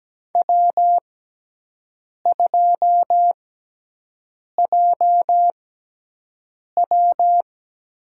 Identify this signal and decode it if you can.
Morse code, heard as a keyed tone, about 17 wpm, spaced as for 6 wpm, 700 Hz. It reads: W2JW
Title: USA Call Signs - 5 Characters - 6x Character Spacing - 17wpm